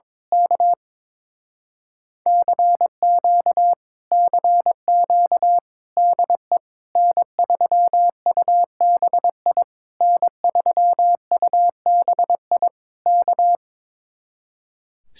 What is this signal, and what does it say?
Morse code, heard as a keyed tone, about 22 wpm, 700 Hz. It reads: K CQ CQ DE N3UBI N3UBI K
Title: Speed-Racing Common QSO Phrases Encore - 22wpm